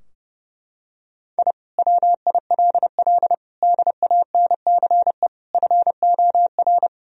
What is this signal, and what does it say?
Morse code, heard as a keyed tone, about 30 wpm, 700 Hz. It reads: I WILL DANCE FOR
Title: Sentences from Top 1000 Words - Review Part 4 - 30wpm